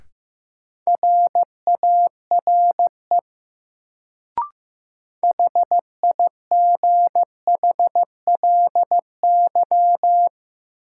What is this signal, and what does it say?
Morse code, heard as a keyed tone, about 15 wpm, 700 Hz. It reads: RARE HIGHLY
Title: Sentences from Top 1100 Words - New Word Drill Alt - 15wpm